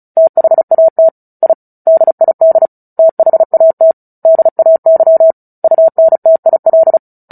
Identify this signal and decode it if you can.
Morse code, heard as a keyed tone, about 34 wpm, 650 Hz. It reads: THAT I DID THAT DAY UNTIL
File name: Book Two / 09-Chapter 9 - 35 WPM